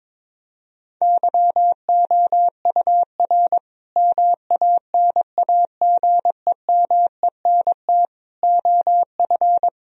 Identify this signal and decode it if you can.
Morse code, heard as a keyed tone, about 22 wpm, 700 Hz. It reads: YOUR MANAGEMENT OF
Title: Sentences from Top 1000 Words - Review Part 4 - 22wpm